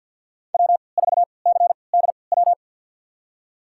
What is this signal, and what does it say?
Morse code, heard as a keyed tone, about 52 wpm, spaced as for 29 wpm, 700 Hz. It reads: W4CDW
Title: USA Call Signs - 5 Characters - 3x Character Spacing - 50wpm